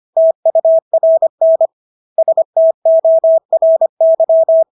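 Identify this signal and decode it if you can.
Morse code, heard as a keyed tone, about 25 wpm, 650 Hz. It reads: TURN STORY
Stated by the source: Morse practice of English words at 25 wpm from CW med Gnister.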